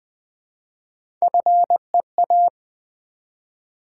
Morse code, heard as a keyed tone, about 20 wpm, 700 Hz. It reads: FEA
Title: Build A Word - 1 Letter At A Time - 4-Letter Words - 20wpm